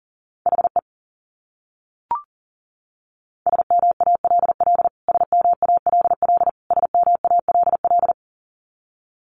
Morse code, heard as a keyed tone, about 38 wpm, 700 Hz. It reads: HE SMALL SMALL SMALL
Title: Top 300 Words Repeated 40wpm